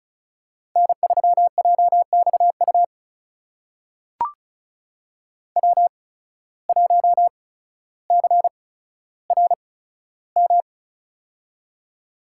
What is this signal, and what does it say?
Morse code, heard as a keyed tone, about 36 wpm, spaced as for 15 wpm, 690 Hz. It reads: N3JXU W1CRM
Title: USA Call Signs - 5 Characters - 8x Character Spacing - 35wpm